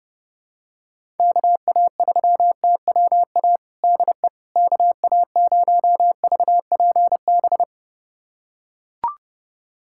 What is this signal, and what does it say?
Morse code, heard as a keyed tone, about 30 wpm, 700 Hz. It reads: KA3TWA DE KA0VPB
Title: USA Callsign DE Callsign - 6 Characters - 30wpm